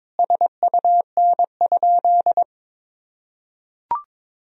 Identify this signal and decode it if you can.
Morse code, heard as a keyed tone, about 22 wpm, 700 Hz. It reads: SUN?
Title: Sentences from Top 1000 Words - Review Part 2 - 22wpm